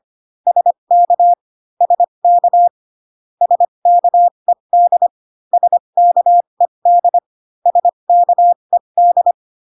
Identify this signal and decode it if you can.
Morse code, heard as a keyed tone, about 25 wpm, spaced as for 22 wpm, 700 Hz. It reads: SK SK SKED SKED SKED
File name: CW common abbreviations alphabetical repeated 3 times 25 18wpm